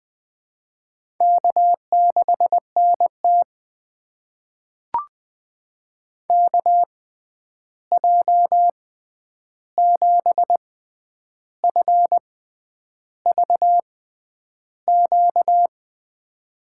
Morse code, heard as a keyed tone, about 20 wpm, spaced as for 9 wpm, 700 Hz. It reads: K6NT KJ7FVQ
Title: USA Call Signs - 6x Character Spacing - 20wpm